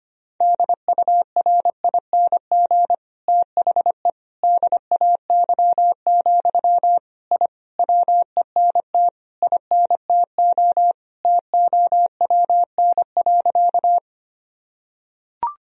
Morse code, heard as a keyed tone, about 25 wpm, 700 Hz. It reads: DURING THE DAY, I WENT INTO TOWN.
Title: Sentences from Top 1000 Words - Review Part 2 - 25wpm